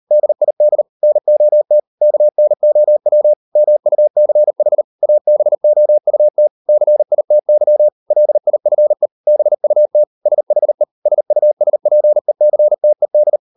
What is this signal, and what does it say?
Morse code, heard as a keyed tone, about 38 wpm, 600 Hz. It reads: DID NOT KNOW MUCH ABOUT CITY LIFE BUT SHE SUSPECTED